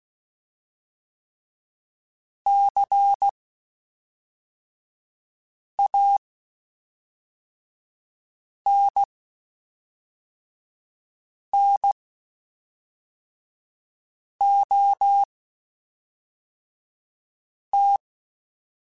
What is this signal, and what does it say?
Morse code, heard as a keyed tone, about 16 wpm, spaced as for 3 wpm, 800 Hz. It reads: CANNOT
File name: Quotes Spaced / Quotes Spaced-mp3s / Advice sp @5